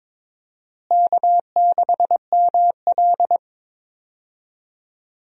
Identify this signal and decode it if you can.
Morse code, heard as a keyed tone, about 22 wpm, 700 Hz. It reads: K6ML
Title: USA Call Signs - 4 Characters - Repeated 2x - 22wpm